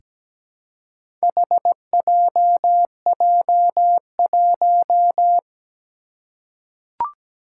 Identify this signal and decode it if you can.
Morse code, heard as a keyed tone, about 17 wpm, 700 Hz. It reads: HJJ1
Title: Four Challenging Letter-Number Combinations 17wpm